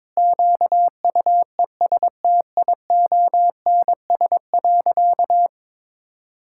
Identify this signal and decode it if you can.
Morse code, heard as a keyed tone, about 22 wpm, 700 Hz. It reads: QUESTIONS.
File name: practice / 22wpm / Sentences with 4-Word Phrases - 22wpm